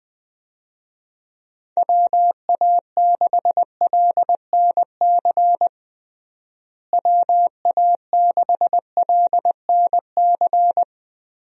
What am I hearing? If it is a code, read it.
Morse code, heard as a keyed tone, about 20 wpm, 700 Hz. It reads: WA6LNC WA6LNC
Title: USA Call Signs - 6 Characters - Repeated 3x - 20wpm